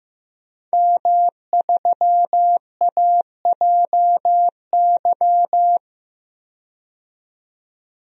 Morse code, heard as a keyed tone, about 15 wpm, 700 Hz. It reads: M3AJY
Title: DXCC-100 Call Signs - 5 Characters - 15wpm